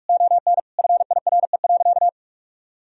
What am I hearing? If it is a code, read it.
Morse code, heard as a keyed tone, about 46 wpm, 700 Hz. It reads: ON FIRE.
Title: Sentences from Top 1000 Words - Review Part 3 - 45wpm